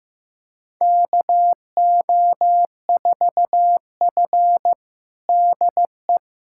Morse code, heard as a keyed tone, about 15 wpm, 700 Hz. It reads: KO4F DE
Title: USA Callsign DE Callsign - 4 Characters - 15wpm